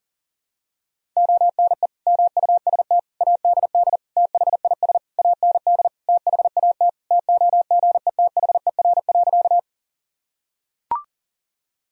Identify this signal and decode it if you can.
Morse code, heard as a keyed tone, about 37 wpm, 700 Hz. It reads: ONE MUST ADD THIS AND THAT TOGETHER.
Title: Sentences from Top 1000 Words - Review Part 1 - 40wpm